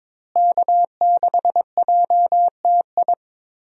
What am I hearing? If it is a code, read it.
Morse code, heard as a keyed tone, about 22 wpm, 700 Hz. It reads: K6JTI
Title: USA Call Signs - 5 Characters - Repeated 3x - 22wpm